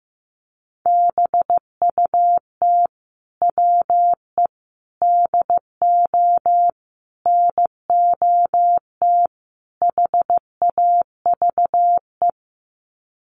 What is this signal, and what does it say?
Morse code, heard as a keyed tone, about 15 wpm, 700 Hz. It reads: BUT WE DO NOT HAVE
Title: Sets of 5 Words Encore2 15wpm